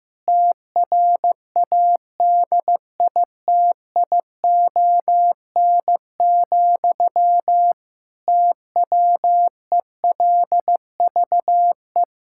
Morse code, heard as a keyed tone, about 15 wpm, 700 Hz. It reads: TRADITION, TWELVE
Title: Pangram Challenge v2 - 15wpm